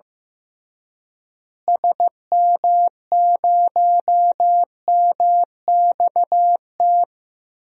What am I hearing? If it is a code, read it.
Morse code, heard as a keyed tone, about 15 wpm, 700 Hz. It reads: SM0MXT
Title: DXCC-100 Call Signs - 6 Characters - Repeated 3x - 15wpm